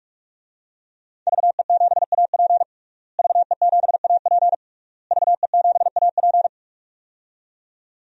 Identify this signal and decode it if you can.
Morse code, heard as a keyed tone, about 46 wpm, 700 Hz. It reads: VE7AP VE7AP VE7AP
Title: DXCC-100 Call Signs - 5 Characters - Repeated 3x - 45wpm